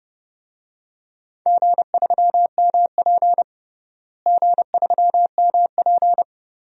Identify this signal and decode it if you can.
Morse code, heard as a keyed tone, about 30 wpm, 700 Hz. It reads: G3MP G3MP
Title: DXCC-100 Call Signs - 4 Characters - Repeated 2x - 30wpm